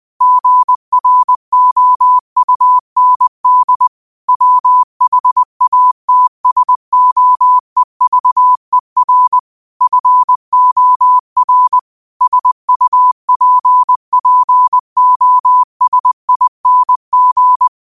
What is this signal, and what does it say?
Morse code, heard as a keyed tone, about 20 wpm, 1000 Hz. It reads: GROUND WHATSOEVER FOR SUPPOSING
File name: qotdfiles / 20wpm